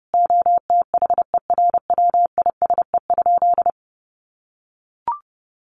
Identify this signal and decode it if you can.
Morse code, heard as a keyed tone, about 28 wpm, 700 Hz. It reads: OTHERWISE?
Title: Sentences from Top 1200 Words 30wpm